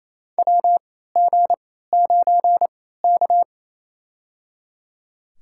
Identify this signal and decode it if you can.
Morse code, heard as a keyed tone, about 28 wpm, spaced as for 16 wpm, 700 Hz. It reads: WG9K
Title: USA Call Signs - 3x Character Spacing - 28wpm